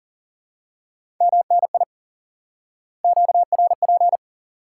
Morse code, heard as a keyed tone, about 40 wpm, 700 Hz. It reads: MNI QRP
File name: IQE Course 3 - Lesson 89 - YAGI - Two Elements - 40wpm